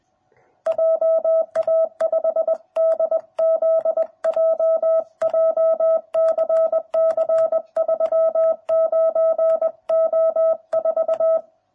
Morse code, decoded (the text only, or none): JA5DZJ JCC 39O4